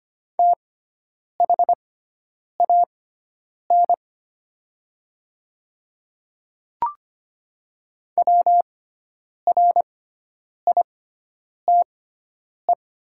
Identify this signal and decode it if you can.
Morse code, heard as a keyed tone, about 25 wpm, spaced as for 9 wpm, 700 Hz. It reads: THAN WRITE
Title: 6x Character Spacing - Top 100 Words - 25wpm